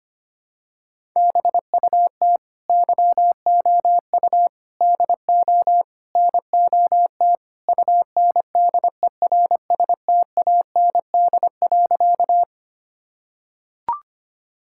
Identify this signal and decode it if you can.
Morse code, heard as a keyed tone, about 25 wpm, 700 Hz. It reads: BUT YOU DO NOT UNDERSTAND.
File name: practice / 25wpm / Sentences with 4-Word Phrases - 25wpm